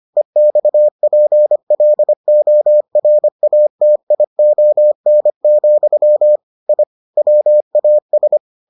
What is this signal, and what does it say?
Morse code, heard as a keyed tone, about 25 wpm, 600 Hz. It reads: EXPLORATION, I WAS